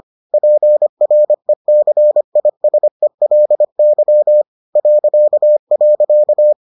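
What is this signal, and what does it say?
Morse code, heard as a keyed tone, about 25 wpm, 600 Hz. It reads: PRECISELY ..
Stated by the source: Morse edition of the Foundations of Amateur Radio episodes.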